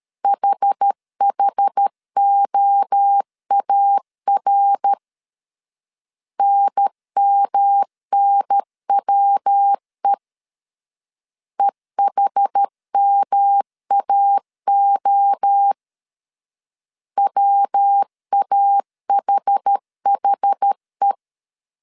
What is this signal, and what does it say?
Morse code, heard as a keyed tone, about 13 wpm, 800 Hz. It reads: HHOAR NMNWE EHMAO WAHHE
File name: CW / DET / 1 3 1500C S B 12wpm 2828